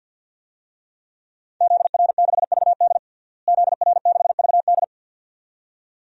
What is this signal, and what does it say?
Morse code, heard as a keyed tone, about 52 wpm, 700 Hz. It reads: ZR6VD ZR6VD
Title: DXCC-100 Call Signs - 5 Characters - Repeated 2x - 50wpm